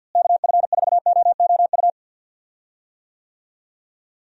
Morse code, heard as a keyed tone, about 52 wpm, 700 Hz. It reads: KF4OOU
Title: USA Call Signs - 6 Characters - Repeated 2x - 50wpm